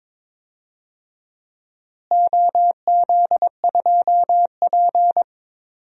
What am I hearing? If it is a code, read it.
Morse code, heard as a keyed tone, about 22 wpm, 700 Hz. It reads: OZ2P
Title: DXCC-100 Call Signs - 4 Characters - Repeated 3x - 22wpm